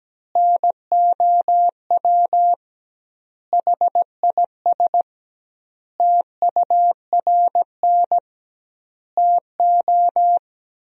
Morse code, heard as a keyed tone, about 17 wpm, 700 Hz. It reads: NOW HIS TURN TO
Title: 2x Word-Spacing - Sentences from Top 200 Words - 17wpm